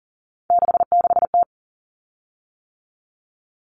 Morse code, heard as a keyed tone, about 40 wpm, 700 Hz. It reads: 66T
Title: Common Cut Numbers - 3-Digit 40wpm